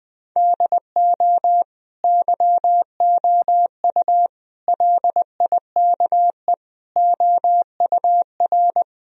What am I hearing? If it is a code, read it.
Morse code, heard as a keyed tone, about 20 wpm, 700 Hz. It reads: DO YOU LIKE OUR
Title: Sentences from Top 1000 Words - Review Part 1 - 20wpm